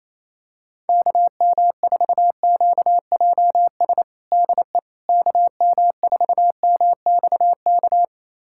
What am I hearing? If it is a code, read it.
Morse code, heard as a keyed tone, about 28 wpm, 700 Hz. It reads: KM4QJS DE KM4MXK
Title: USA Callsign DE Callsign - 6 Characters - 28wpm